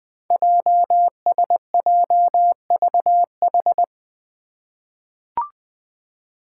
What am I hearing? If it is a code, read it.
Morse code, heard as a keyed tone, about 21 wpm, 700 Hz. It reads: JSJVH
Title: Five Challenging Letter-Number Combinations 20wpm